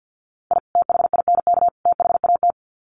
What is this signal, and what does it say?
Morse code, heard as a keyed tone, about 48 wpm, 700 Hz. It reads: I THINK THAT
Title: Sets of 3 Words 50wpm